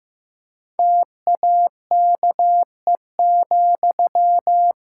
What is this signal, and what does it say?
Morse code, heard as a keyed tone, about 15 wpm, 700 Hz. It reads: TAKE,
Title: Sentences from Top 1000 Words - Review Part 5 - 15wpm